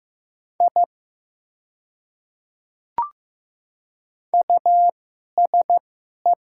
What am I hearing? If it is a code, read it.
Morse code, heard as a keyed tone, about 15 wpm, spaced as for 11 wpm, 700 Hz. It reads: I USE
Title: 2x Character Spacing - Top 100 Words - 15wpm